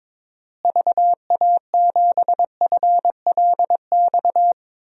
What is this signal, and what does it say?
Morse code, heard as a keyed tone, about 22 wpm, 700 Hz. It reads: VA7FLX
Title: DXCC-100 Call Signs - 6 Characters - Repeated 2x - 22wpm